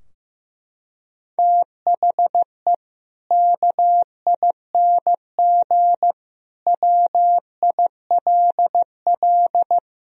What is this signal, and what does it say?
Morse code, heard as a keyed tone, about 15 wpm, 700 Hz. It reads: THE KING WILL
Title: Sentences from Top 1000 Words - Review Part 4 - 15wpm